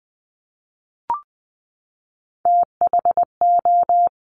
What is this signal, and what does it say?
Morse code, heard as a keyed tone, about 20 wpm, 700 Hz. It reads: THO
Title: Top Three Letter Combinations in Words 20wpm